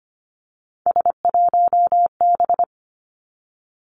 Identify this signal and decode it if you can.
Morse code, heard as a keyed tone, about 25 wpm, 700 Hz. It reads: S1B
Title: Three Challenging Letter-Number Combinations 25wpm